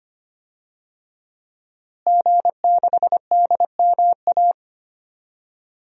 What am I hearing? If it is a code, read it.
Morse code, heard as a keyed tone, about 25 wpm, 700 Hz. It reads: G6DMA
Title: DXCC-100 Call Signs - 5 Characters - Repeated 3x - 25wpm